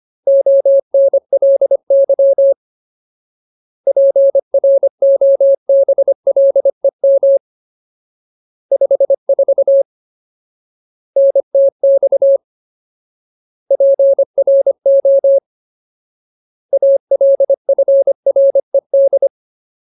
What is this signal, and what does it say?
Morse code, heard as a keyed tone, about 25 wpm, 550 Hz. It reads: ONLY PROBLEM 54 NTX PRO ALFRED
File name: Ultimate Word List / 25WPM Ultimate